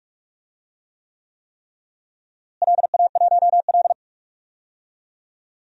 Morse code, heard as a keyed tone, about 48 wpm, 700 Hz. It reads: LA1L
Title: DXCC-100 Call Signs - 4 Characters - Repeated 3x - 45wpm